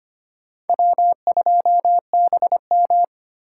Morse code, heard as a keyed tone, about 32 wpm, 700 Hz. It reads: W2BM